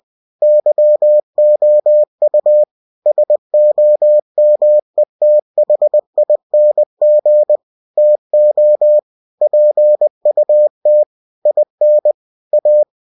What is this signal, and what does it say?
Morse code, heard as a keyed tone, about 20 wpm, 600 Hz. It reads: YOU SOMETHING TO PUT IN A
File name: cw-books / winnie-the-pooh / winnie-20wpm-page-0058